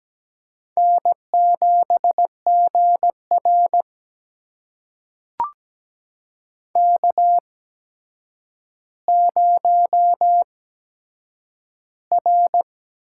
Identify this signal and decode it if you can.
Morse code, heard as a keyed tone, about 17 wpm, spaced as for 8 wpm, 700 Hz. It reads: N7GR K0R
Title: USA Call Signs - 4 Characters - 8x Character Spacing - 17wpm